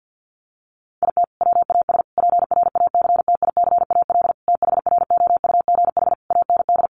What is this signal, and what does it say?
Morse code, heard as a keyed tone, about 50 wpm, 700 Hz. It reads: IT WAS PRACTICAL THROUGH AND
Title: Sentences from Top 1400 Words 50wpm